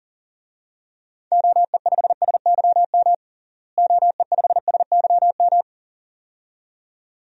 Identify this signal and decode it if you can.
Morse code, heard as a keyed tone, about 40 wpm, 700 Hz. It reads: OE5SYM OE5SYM
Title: DXCC-100 Call Signs - 6 Characters - Repeated 2x - 40wpm